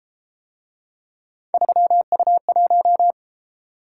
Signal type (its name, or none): Morse code